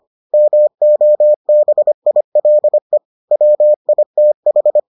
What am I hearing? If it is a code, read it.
Morse code, heard as a keyed tone, about 25 wpm, 600 Hz. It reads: MOBILE WITH